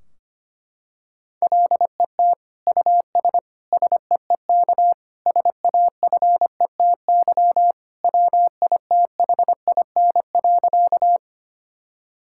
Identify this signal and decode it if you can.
Morse code, heard as a keyed tone, about 25 wpm, 700 Hz. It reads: LET US SEEK SAFETY WITHIN.
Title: Sentences from Top 1600 Words 25wpm